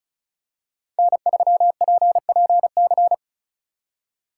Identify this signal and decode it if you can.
Morse code, heard as a keyed tone, about 36 wpm, 700 Hz. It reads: N3PPC
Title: USA Call Signs - 5 Characters - 2x Character Spacing - 35wpm